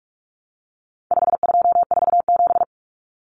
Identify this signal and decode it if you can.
Morse code, heard as a keyed tone, about 44 wpm, 700 Hz. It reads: 5247
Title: Common Cut Numbers - 4-Digit 45wpm